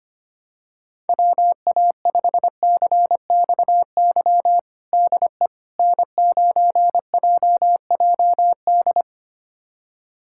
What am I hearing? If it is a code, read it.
Morse code, heard as a keyed tone, about 25 wpm, 700 Hz. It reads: WA5CXY DE N9JJD
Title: USA Callsign DE Callsign 25wpm